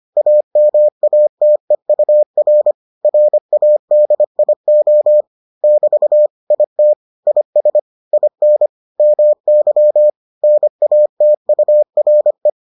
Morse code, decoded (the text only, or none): AMATEUR RADIO <BT> IT IS IN MY NATURE